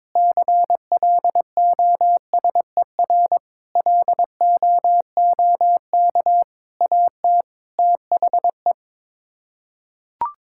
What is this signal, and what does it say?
Morse code, heard as a keyed tone, about 22 wpm, 700 Hz. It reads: CLOSER LOOK AT THE
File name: practice / 22wpm / Sets of 6 Words - 22wpm